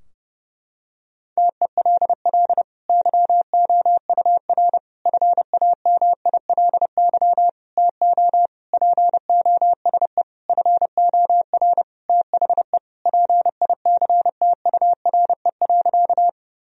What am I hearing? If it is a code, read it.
Morse code, heard as a keyed tone, about 32 wpm, 700 Hz. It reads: TELL YOUR FAMILY TO POSE FOR THE PICTURE.
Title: Sentences from Top 1000 Words - Review Part 2 - 30wpm